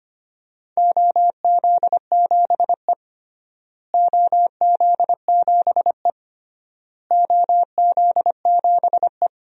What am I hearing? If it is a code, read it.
Morse code, heard as a keyed tone, about 25 wpm, 700 Hz. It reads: OZ7E OZ7E OZ7E
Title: DXCC-100 Call Signs - 4 Characters - Repeated 3x - 25wpm